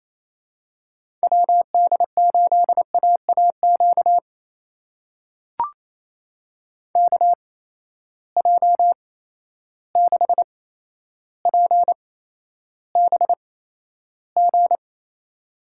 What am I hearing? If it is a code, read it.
Morse code, heard as a keyed tone, about 28 wpm, spaced as for 12 wpm, 700 Hz. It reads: WD8AAQ KJ6PBG
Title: USA Call Signs - 6 Characters - 8x Character Spacing - 28wpm